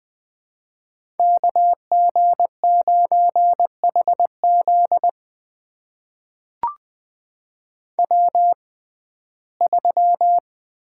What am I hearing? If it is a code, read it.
Morse code, heard as a keyed tone, about 20 wpm, spaced as for 15 wpm, 700 Hz. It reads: KG9HZ W3
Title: USA Call Signs - 5 Characters - 6x Character Spacing - 20wpm